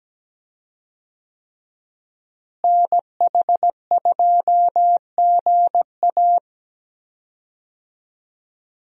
Morse code, heard as a keyed tone, about 17 wpm, 700 Hz. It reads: NH2GA
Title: USA Call Signs - 5 Characters - Repeated 3x - 17wpm